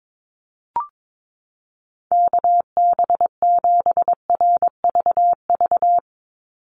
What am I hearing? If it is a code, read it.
Morse code, heard as a keyed tone, about 22 wpm, 700 Hz. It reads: KB7RVV
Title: USA Call Signs 22wpm